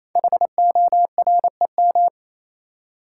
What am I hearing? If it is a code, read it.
Morse code, heard as a keyed tone, about 29 wpm, 700 Hz. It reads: HOREM
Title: Code Groups - 5 Letters - 28wpm